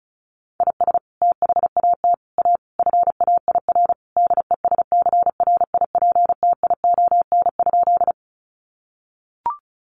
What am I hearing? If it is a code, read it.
Morse code, heard as a keyed tone, about 34 wpm, 700 Hz. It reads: IS THAT A FAIR DESCRIPTION?
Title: Sentences from Top 1300 Words 35wpm